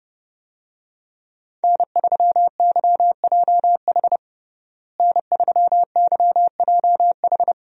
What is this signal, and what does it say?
Morse code, heard as a keyed tone, about 32 wpm, 700 Hz. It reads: N3YJH N3YJH
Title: USA Call Signs - 5 Characters - Repeated 3x - 30wpm